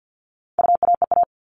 Morse code, heard as a keyed tone, about 50 wpm, 700 Hz. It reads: UAEA
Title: All Cut Numbers - 4-Digit 50wpm